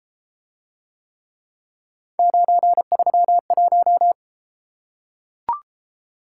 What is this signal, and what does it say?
Morse code, heard as a keyed tone, about 31 wpm, 700 Hz. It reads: 931